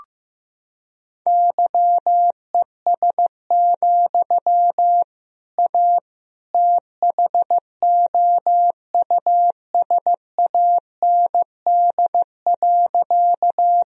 Morse code, heard as a keyed tone, about 15 wpm, 700 Hz. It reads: YES, A THOUSAND.
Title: Sentences from Top 1000 Words - Review Part 3 - 15wpm